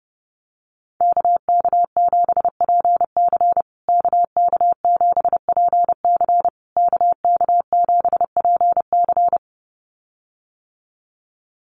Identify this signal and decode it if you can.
Morse code, heard as a keyed tone, about 30 wpm, 700 Hz. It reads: KK7PC KK7PC KK7PC
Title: USA Call Signs Repeated 3x 30wpm